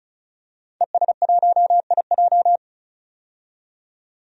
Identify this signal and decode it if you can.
Morse code, heard as a keyed tone, about 36 wpm, 700 Hz. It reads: ES1IJ
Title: DXCC-100 Call Signs - 5 Characters - 35wpm